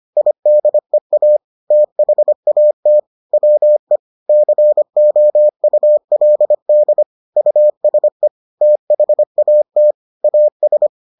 Morse code, heard as a keyed tone, about 25 wpm, 600 Hz. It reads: IDEA THAT WE COULD USE THAT AS